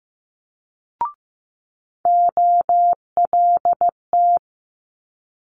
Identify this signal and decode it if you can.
Morse code, heard as a keyed tone, about 15 wpm, 700 Hz. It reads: OLT
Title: Multi-letter Phonemes Quick Fire 15wpm